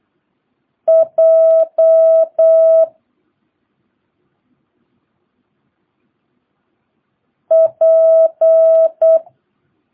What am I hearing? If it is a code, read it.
Morse code, heard as a keyed tone, about 8 wpm, 650 Hz. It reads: J P
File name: jh8cbh / 2022cw